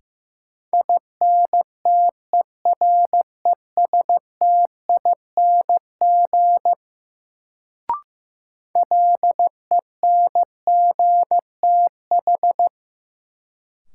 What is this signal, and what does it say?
Morse code, heard as a keyed tone, about 15 wpm, 700 Hz. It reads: INTERESTING LENGTH
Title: Sentences from Top 1000 Words - New Word Drill - 15wpm